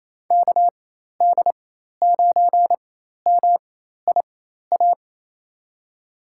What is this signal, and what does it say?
Morse code, heard as a keyed tone, about 28 wpm, spaced as for 13 wpm, 700 Hz. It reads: KD9MIA
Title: USA Call Signs - 6 Characters - 4x Character Spacing - 28wpm